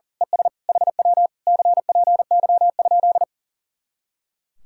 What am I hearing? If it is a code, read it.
Morse code, heard as a keyed tone, about 37 wpm, 700 Hz. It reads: ES HW CPY?
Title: Speed-Racing Common QSO Phrases - 40wpm